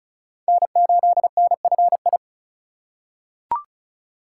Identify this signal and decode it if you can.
Morse code, heard as a keyed tone, about 36 wpm, 700 Hz. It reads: N8NFI
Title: USA Call Signs - 5 Characters - 2x Character Spacing - 35wpm